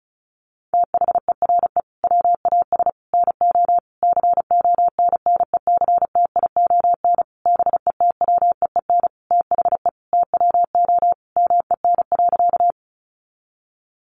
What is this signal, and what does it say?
Morse code, heard as a keyed tone, about 34 wpm, 700 Hz. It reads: THERE WAS NO CONNECTION BETWEEN THE TWO MEN.
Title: Sentences from Top 1300 Words 35wpm